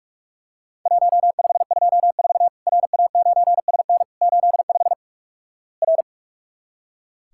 Morse code, heard as a keyed tone, about 46 wpm, 700 Hz. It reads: 1524 RA9SN 85 R
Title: Worked All Europe DX Contest - QTC Exchange - 45wpm